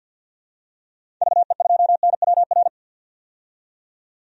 Morse code, heard as a keyed tone, about 47 wpm, 700 Hz. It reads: VE2NWR
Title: DXCC-100 Call Signs - 6 Characters - 50wpm